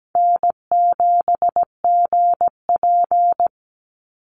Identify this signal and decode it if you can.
Morse code, heard as a keyed tone, about 17 wpm, 700 Hz. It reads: N7GP
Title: USA Call Signs 17wpm